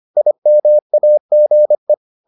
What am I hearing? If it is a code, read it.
Morse code, heard as a keyed tone, about 25 wpm, 600 Hz. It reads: IMAGE